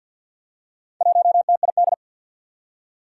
Morse code, heard as a keyed tone, about 52 wpm, 700 Hz. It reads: 1TID
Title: Four Challenging Letter-Number Combinations 50wpm